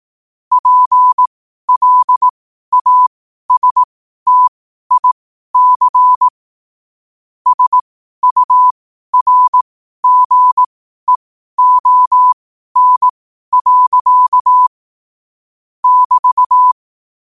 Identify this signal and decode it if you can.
Morse code, heard as a keyed tone, about 18 wpm, spaced as for 12 wpm, 1000 Hz. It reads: PLASTIC SURGEON. <BT>
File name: qotdfiles / 12wpm